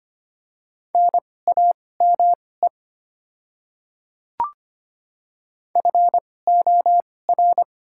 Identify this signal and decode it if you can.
Morse code, heard as a keyed tone, about 25 wpm, spaced as for 18 wpm, 690 Hz. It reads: NAME FOR
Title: 2x Character Spacing - Top 200 Words - 25wpm